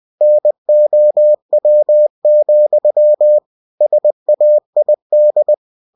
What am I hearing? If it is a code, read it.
Morse code, heard as a keyed tone, about 20 wpm, 600 Hz. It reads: NOW, SAID
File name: cw-books / winnie-the-pooh / winnie-20wpm-page-0033